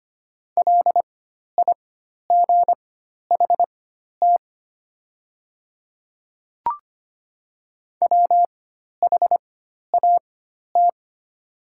Morse code, heard as a keyed tone, about 25 wpm, spaced as for 12 wpm, 700 Hz. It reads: LIGHT WHAT
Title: 4x Character Spacing - Top 300 Words - 25wpm